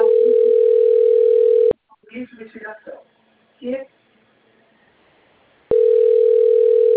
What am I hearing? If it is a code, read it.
Ringback tone (North America), 440 + 480 Hz, with speech.
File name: out- - - - - .10911